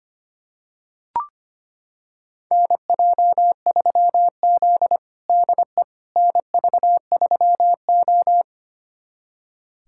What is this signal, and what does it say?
Morse code, heard as a keyed tone, about 25 wpm, 700 Hz. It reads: NJ3Z DE NV3O
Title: USA Callsign DE Callsign - 4 Characters - 25wpm